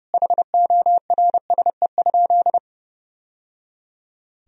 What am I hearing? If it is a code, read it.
Morse code, heard as a keyed tone, about 32 wpm, 700 Hz. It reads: HORSE?
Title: Sentences from Top 1000 Words - Review Part 1 - 30wpm